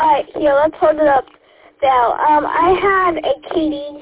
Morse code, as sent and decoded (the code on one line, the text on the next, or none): none